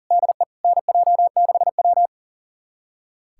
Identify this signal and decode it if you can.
Morse code, heard as a keyed tone, about 40 wpm, 700 Hz. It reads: DE NJ6W
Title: USA Callsign DE Callsign 40wpm